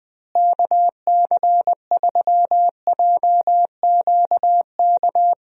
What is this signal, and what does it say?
Morse code, heard as a keyed tone, about 20 wpm, 700 Hz. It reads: KC3JQK
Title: USA Call Signs - 6 Characters - Repeated 3x - 20wpm